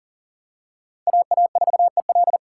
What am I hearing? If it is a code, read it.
Morse code, heard as a keyed tone, about 40 wpm, 700 Hz. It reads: AA4EL
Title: USA Call Signs - 2x Character Spacing - 40wpm